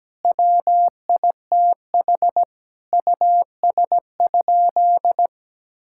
Morse code, heard as a keyed tone, about 17 wpm, 700 Hz. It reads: WITH US?
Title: Sentences from Top 1000 Words - Review Part 6 - 17wpm